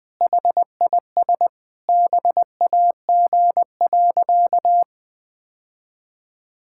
Morse code, heard as a keyed tone, about 21 wpm, 700 Hz. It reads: HIS BAG.
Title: Sentences with 3-Word Phrases - 20wpm